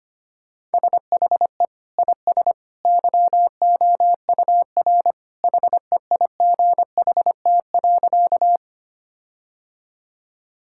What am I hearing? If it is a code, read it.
Morse code, heard as a keyed tone, about 25 wpm, 700 Hz. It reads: SHE IS YOUR HEIGHT.
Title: Sentences from Top 1600 Words 25wpm